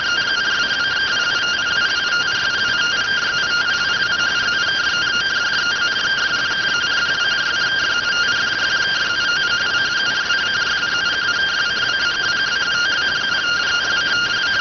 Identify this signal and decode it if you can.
Two-tone teleprinter signal, about 170 Hz shift.